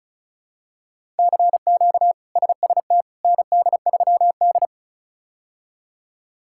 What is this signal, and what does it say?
Morse code, heard as a keyed tone, about 36 wpm, 700 Hz. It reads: CQ SST ND3D
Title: K1USN SST QSO Practice - 35wpm